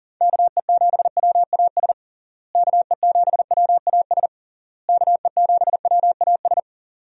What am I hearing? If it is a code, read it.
Morse code, heard as a keyed tone, about 40 wpm, 700 Hz. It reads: KE7WAS KE7WAS KE7WAS
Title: USA Call Signs - 6 Characters - Repeated 3x - 40wpm